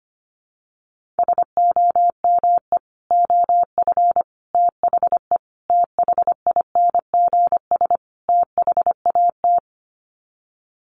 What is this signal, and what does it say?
Morse code, heard as a keyed tone, about 25 wpm, 700 Hz. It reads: SOME OF THE THINGS THAT
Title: Sets of 5 Words 25wpm